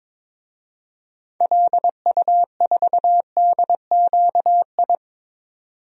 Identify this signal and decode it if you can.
Morse code, heard as a keyed tone, about 21 wpm, 700 Hz. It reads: LU4DQI
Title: DXCC-100 Call Signs - 6 Characters - 22wpm